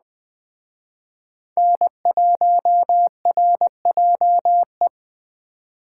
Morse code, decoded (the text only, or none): N1RJE